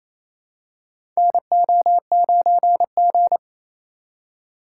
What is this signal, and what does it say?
Morse code, heard as a keyed tone, about 28 wpm, 700 Hz. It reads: NO9G
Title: USA Call Signs - 4 Characters - 3x Character Spacing - 28wpm